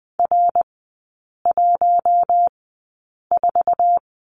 Morse code, heard as a keyed tone, about 20 wpm, spaced as for 8 wpm, 700 Hz. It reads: R14
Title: 053 Lesson 10e - Letter r - Three Characters 20wpm